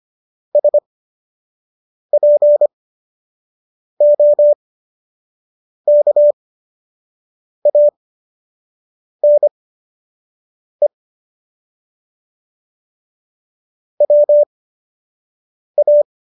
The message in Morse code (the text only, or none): SPOKANE WA